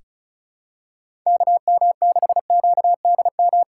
Morse code, heard as a keyed tone, about 34 wpm, 700 Hz. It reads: KM6QDM
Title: USA Call Signs - 6 Characters - 3x Character Spacing - 35wpm